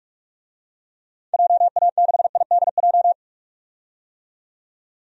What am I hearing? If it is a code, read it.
Morse code, heard as a keyed tone, about 46 wpm, 700 Hz. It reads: JA6IDJ
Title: DXCC-100 Call Signs - 6 Characters - 45wpm